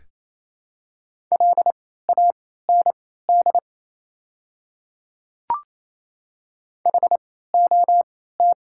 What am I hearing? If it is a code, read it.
Morse code, heard as a keyed tone, about 28 wpm, spaced as for 16 wpm, 700 Hz. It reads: LAND HOT
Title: 3x Character Spacing - Top 200 Words - 28wpm